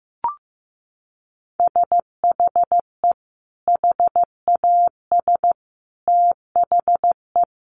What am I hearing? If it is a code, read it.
Morse code, heard as a keyed tone, about 15 wpm, 700 Hz. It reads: SHE HAS THE
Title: Sentences from Top 900 Words 15wpm